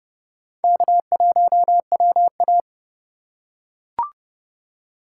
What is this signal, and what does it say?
Morse code, heard as a keyed tone, about 30 wpm, 700 Hz. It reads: K1WA
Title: USA Call Signs - 4x Character Spacing - 30wpm